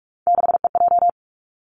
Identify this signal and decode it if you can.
Morse code, heard as a keyed tone, about 46 wpm, 700 Hz. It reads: 6EJ
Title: Three Challenging Letter-Number Combinations 45wpm